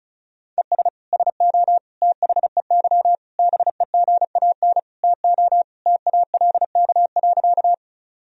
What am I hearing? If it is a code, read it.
Morse code, heard as a keyed tone, about 36 wpm, 700 Hz. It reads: ES SO THEY BEGAN TO TALK.
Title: Sentences from Top 400 Words and Abbreviations 35wpm